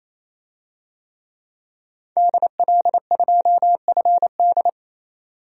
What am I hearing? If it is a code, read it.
Morse code, heard as a keyed tone, about 28 wpm, 700 Hz. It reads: DL2FD
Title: DXCC-100 Call Signs - 5 Characters - 28wpm